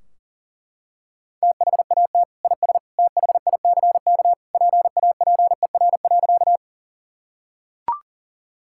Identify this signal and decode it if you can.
Morse code, heard as a keyed tone, about 37 wpm, 700 Hz. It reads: THAT IS THICK PAPER.
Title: Sentences from Top 1600 Words 40wpm